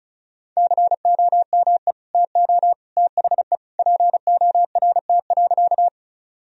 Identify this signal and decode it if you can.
Morse code, heard as a keyed tone, about 36 wpm, 700 Hz. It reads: COME TO THE PORT.
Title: Sentences from Top 1000 Words - Review Part 1 - 35wpm